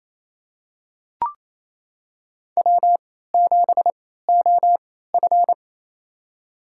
Morse code, heard as a keyed tone, about 28 wpm, spaced as for 16 wpm, 700 Hz. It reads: W7OF
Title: USA Call Signs - 4 Characters - 3x Character Spacing - 28wpm